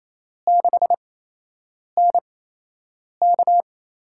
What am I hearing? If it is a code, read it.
Morse code, heard as a keyed tone, about 28 wpm, spaced as for 8 wpm, 700 Hz. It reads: 6NK
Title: USA Call Signs - 4 Characters - 8x Character Spacing - 28wpm